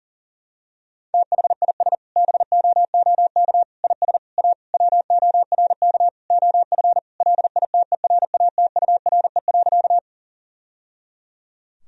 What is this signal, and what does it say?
Morse code, heard as a keyed tone, about 40 wpm, 700 Hz. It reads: THIS BOOK IS A WORK OF LITERATURE.
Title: Sentences from Top 1000 Words - Review Part 3 - 40wpm